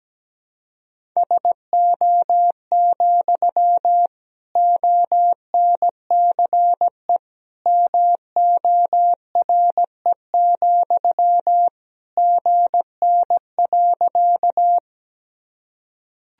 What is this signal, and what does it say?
Morse code, heard as a keyed tone, about 17 wpm, 700 Hz. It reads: SO, ONCE MORE, GN.
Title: Sentences from Top 300 Words and Abbreviations 17wpm